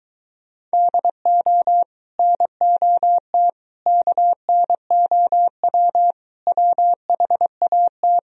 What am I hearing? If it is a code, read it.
Morse code, heard as a keyed tone, about 23 wpm, 700 Hz. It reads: DO NOT KNOW WHAT